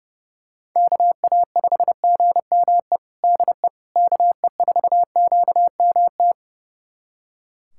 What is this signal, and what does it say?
Morse code, heard as a keyed tone, about 30 wpm, 700 Hz. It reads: KA5GME DE KE4QMT
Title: USA Callsign DE Callsign - 6 Characters - 30wpm